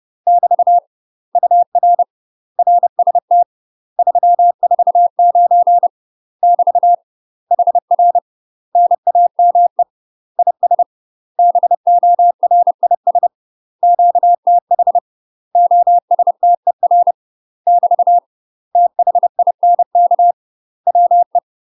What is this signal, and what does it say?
Morse code, heard as a keyed tone, about 30 wpm, 700 Hz. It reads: <BT> UR RST 349 <BT> HR NAME IS BORIS QTH OSTER <BT> THINK WE
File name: UK MASTER QSOs TRAINING / 30WPM / UK MASTER QSOs 0027